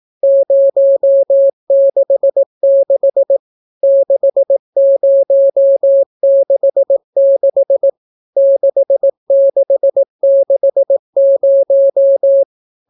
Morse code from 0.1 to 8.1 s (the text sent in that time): 066 6066